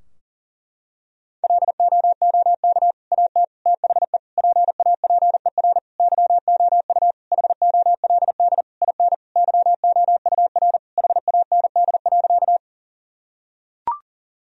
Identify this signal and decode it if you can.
Morse code, heard as a keyed tone, about 40 wpm, 700 Hz. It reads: LOOK AT THE PAPER YOU HOLD IN YOUR HAND.
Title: Sentences from Top 1000 Words - Review Part 2 - 40wpm